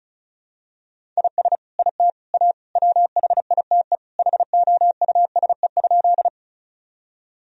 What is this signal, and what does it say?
Morse code, heard as a keyed tone, about 36 wpm, 700 Hz. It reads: IS IT A WHITE HOUSE?
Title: Sentences from Top 1000 Words - Review Part 1 - 35wpm